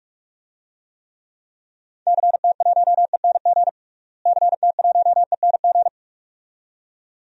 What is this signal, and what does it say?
Morse code, heard as a keyed tone, about 46 wpm, 700 Hz. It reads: CT1ENG CT1ENG
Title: DXCC-100 Call Signs - 6 Characters - Repeated 2x - 45wpm